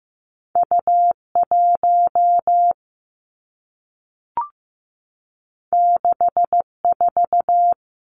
Morse code, heard as a keyed tone, about 15 wpm, 700 Hz. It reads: U1 64
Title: Two Challenging Letter-Number Combinations 15wpm